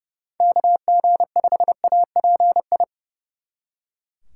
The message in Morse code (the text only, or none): KG5API